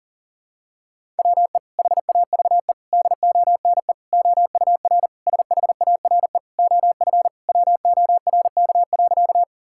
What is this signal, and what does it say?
Morse code, heard as a keyed tone, about 40 wpm, 700 Hz. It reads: WE HAVE DONE OUR SHARE OF WORK.
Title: Sentences from Top 1000 Words - Review Part 6 - 40wpm